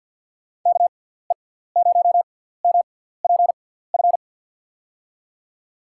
Morse code, heard as a keyed tone, about 50 wpm, spaced as for 17 wpm, 700 Hz. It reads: KE0MPF
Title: USA Call Signs - 6 Characters - 6x Character Spacing - 50wpm